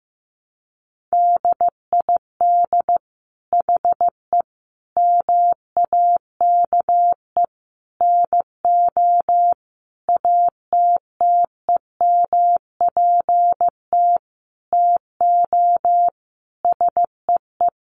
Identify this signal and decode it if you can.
Morse code, heard as a keyed tone, about 15 wpm, 700 Hz. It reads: DID HE MAKE NO ATTEMPT TO SEE
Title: Sentences from Top 1000 Words 15wpm